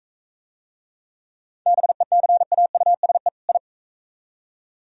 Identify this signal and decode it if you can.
Morse code, heard as a keyed tone, about 42 wpm, 700 Hz. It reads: BECAUSE I